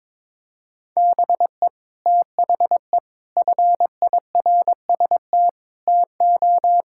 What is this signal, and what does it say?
Morse code, heard as a keyed tone, about 22 wpm, 700 Hz. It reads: BE THE FIRST TO
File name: practice / 22wpm / Sets of 6 Words - 22wpm